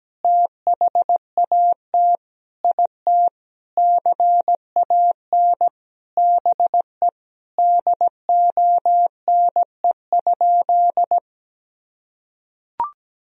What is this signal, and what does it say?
Morse code, heard as a keyed tone, about 17 wpm, 700 Hz. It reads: THAT IT CAN BE DONE?
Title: Sentences from Top 1000 Words - Review Part 1 - 17wpm